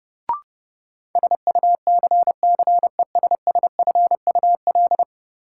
Morse code, heard as a keyed tone, about 31 wpm, 700 Hz. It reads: SUCCESSFUL
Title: Sentences from Top 900 Words - New Word Drill - 30wpm